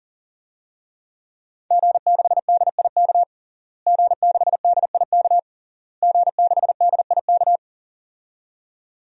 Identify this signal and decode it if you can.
Morse code, heard as a keyed tone, about 40 wpm, 700 Hz. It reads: G6DIK G6DIK G6DIK
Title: DXCC-100 Call Signs - 5 Characters - Repeated 3x - 40wpm